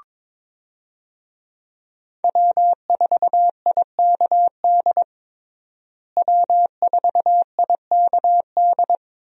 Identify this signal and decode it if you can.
Morse code, heard as a keyed tone, about 21 wpm, 700 Hz. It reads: W4IKD W4IKD
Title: USA Call Signs - 5 Characters - Repeated 2x - 22wpm